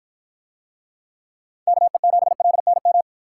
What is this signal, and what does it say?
Morse code, heard as a keyed tone, about 49 wpm, 700 Hz. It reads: KE7LNM